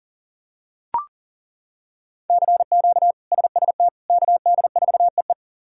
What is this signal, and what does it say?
Morse code, heard as a keyed tone, about 37 wpm, 700 Hz. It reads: CQ SST KD4EE
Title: K1USN SST QSO Practice - 40wpm